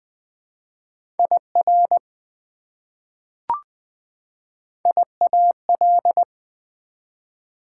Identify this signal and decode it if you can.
Morse code, heard as a keyed tone, about 20 wpm, 700 Hz. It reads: IR IAL
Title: Prefixes and Suffixes 20wpm